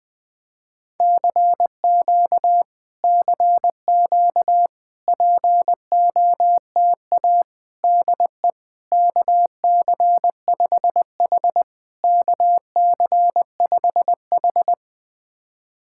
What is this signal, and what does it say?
Morse code, heard as a keyed tone, about 20 wpm, 700 Hz. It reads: CQ CQ POTA DE KC5H KC5H
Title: POTA QSO Practice - 20wpm